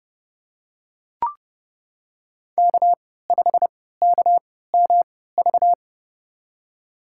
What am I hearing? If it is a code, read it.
Morse code, heard as a keyed tone, about 32 wpm, spaced as for 17 wpm, 700 Hz. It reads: K5KMV
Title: USA Call Signs - 3x Character Spacing - 30wpm